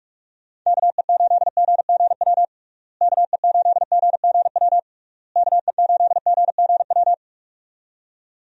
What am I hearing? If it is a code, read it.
Morse code, heard as a keyed tone, about 46 wpm, 700 Hz. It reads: KE8GGW KE8GGW KE8GGW
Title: USA Call Signs - 6 Characters - Repeated 3x - 45wpm